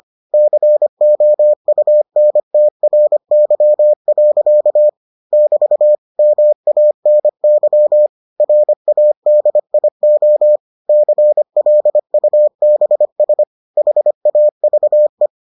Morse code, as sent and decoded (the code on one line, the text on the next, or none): -.-. --- ..- -. - .-. -.-- .-.-.- / -...- / -- .- -. -.-- / .-. .- -.. .. --- / -.-. .-.. ..- -... ... / .... .- ...- .
COUNTRY. <BT> MANY RADIO CLUBS HAVE